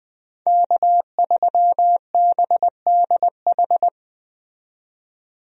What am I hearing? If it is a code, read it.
Morse code, heard as a keyed tone, about 20 wpm, 700 Hz. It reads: K3BDH
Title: USA Call Signs - 5 Characters - 20wpm